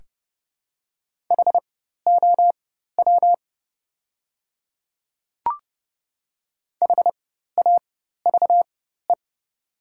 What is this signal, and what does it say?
Morse code, heard as a keyed tone, about 32 wpm, spaced as for 14 wpm, 700 Hz. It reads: HOW HAVE
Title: 4x Character Spacing - Top 200 Words - 30wpm